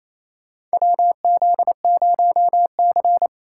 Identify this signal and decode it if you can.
Morse code, heard as a keyed tone, about 28 wpm, 700 Hz. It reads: WZ0C
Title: USA Call Signs - 4 Characters - 2x Character Spacing - 28wpm